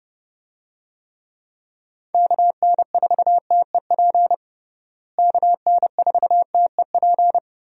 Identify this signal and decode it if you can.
Morse code, heard as a keyed tone, about 30 wpm, 700 Hz. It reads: KN4TEP KN4TEP
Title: USA Call Signs - 6 Characters - Repeated 3x - 30wpm